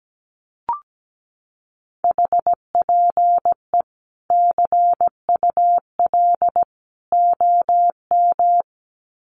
Morse code, heard as a keyed tone, about 17 wpm, 700 Hz. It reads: HPE CUL OM
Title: Common QSO Phrases 17wpm